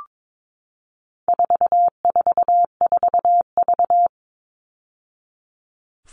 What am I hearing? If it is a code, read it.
Morse code, heard as a keyed tone, about 22 wpm, 700 Hz. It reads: 444V
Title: All Cut Numbers - 4-Digit 22wpm